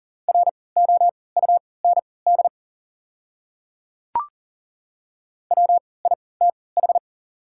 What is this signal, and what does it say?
Morse code, heard as a keyed tone, about 40 wpm, spaced as for 23 wpm, 700 Hz. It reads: ROUND WITH
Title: 3x Character Spacing - Top 200 Words - 40wpm